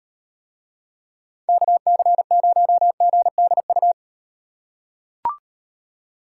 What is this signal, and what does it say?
Morse code, heard as a keyed tone, about 36 wpm, 700 Hz. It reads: KC0GDU